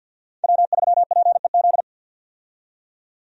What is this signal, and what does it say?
Morse code, heard as a keyed tone, about 46 wpm, 700 Hz. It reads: W3PEZ
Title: USA Call Signs - 5 Characters - 50wpm